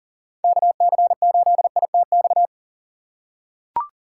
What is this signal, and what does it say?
Morse code, heard as a keyed tone, about 40 wpm, 700 Hz. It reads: KC8ITX
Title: USA Call Signs - 6x Character Spacing - 40wpm